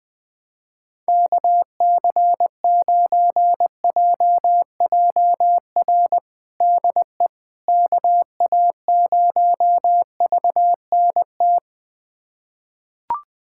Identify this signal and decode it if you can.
Morse code, heard as a keyed tone, about 21 wpm, 700 Hz. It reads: KC9JJR DE KA0VNT
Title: USA Callsign DE Callsign - 6 Characters - 20wpm